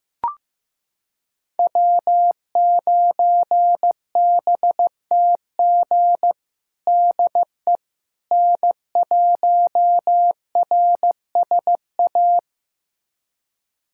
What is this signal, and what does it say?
Morse code, heard as a keyed tone, about 16 wpm, 700 Hz. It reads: W9BTG DE N1RSA
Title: USA Callsign DE Callsign - 5 Characters - 15wpm